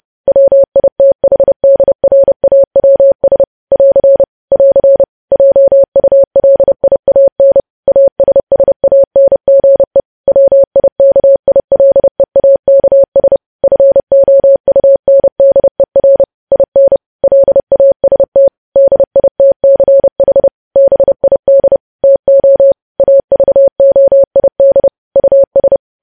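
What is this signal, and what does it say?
Morse code, heard as a keyed tone, about 30 wpm, 550 Hz. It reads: WITHDRAWS <AR> <AR> JULIAN ASSANGE WIKILEAKS FOUNDER IN LAST DITCH BID TO AVOID US
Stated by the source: Morse news headlines sent at 30 words per minute.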